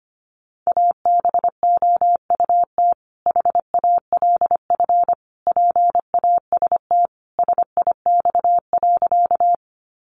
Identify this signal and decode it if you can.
Morse code, heard as a keyed tone, about 25 wpm, 700 Hz. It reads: ABOUT HALF PAST SIX.
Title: Sentences from Top 600 Words 25wpm